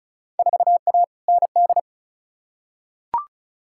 Morse code, heard as a keyed tone, about 36 wpm, 690 Hz. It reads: 4A ND
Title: Field Day Exchanges Encore - 35wpm